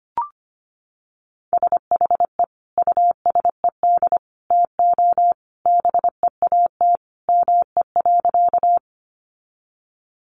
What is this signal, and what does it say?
Morse code, heard as a keyed tone, about 25 wpm, 700 Hz. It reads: SHE USED TO BEAT ME.
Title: Sentences from Top 1300 Words 25wpm